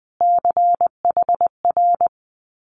Morse code, heard as a keyed tone, about 20 wpm, 700 Hz. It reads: CHR
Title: Multi-letter Phonemes Repeated 3x 20wpm